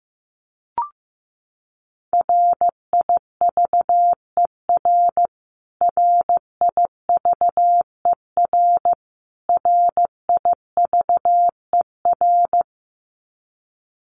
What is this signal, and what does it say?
Morse code, heard as a keyed tone, about 15 wpm, 700 Hz. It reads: RIVER RIVER RIVER
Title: Top 500 Words Repeated 15wpm